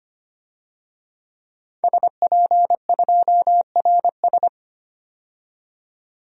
Morse code, heard as a keyed tone, about 25 wpm, 700 Hz. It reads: SP2RS
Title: DXCC-100 Call Signs - 5 Characters - 25wpm